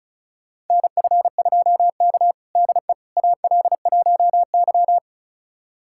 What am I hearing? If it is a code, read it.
Morse code, heard as a keyed tone, about 36 wpm, 700 Hz. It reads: NF2K DE AL1Y
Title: USA Callsign DE Callsign - 4 Characters - 35wpm